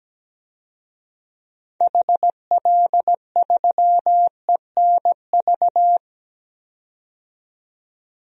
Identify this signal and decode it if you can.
Morse code, heard as a keyed tone, about 17 wpm, 700 Hz. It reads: HL3ENV
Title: DXCC-100 Call Signs - 6 Characters - 17wpm